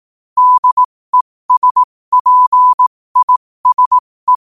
Morse code, heard as a keyed tone, about 18 wpm, spaced as for 15 wpm, 1000 Hz. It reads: DESPISE